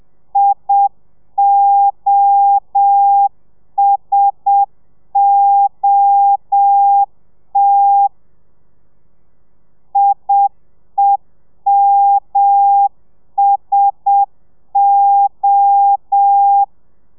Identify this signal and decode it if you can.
Morse code, heard as a keyed tone, about 7 wpm, 800 Hz. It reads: IOSOT IEMSO
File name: telegrafia / i2tfj / Lez1 30